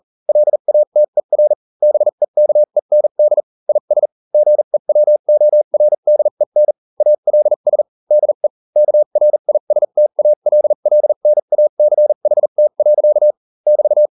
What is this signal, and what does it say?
Morse code, heard as a keyed tone, about 40 wpm, spaced as for 35 wpm, 600 Hz. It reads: LATER BEKEND IS GEWORDEN ALS DE KRISTALLNACHT. <BT>